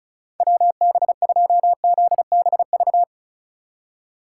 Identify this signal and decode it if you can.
Morse code, heard as a keyed tone, about 36 wpm, 700 Hz. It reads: WB2ZBV
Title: USA Call Signs - 6 Characters - 4x Character Spacing - 35wpm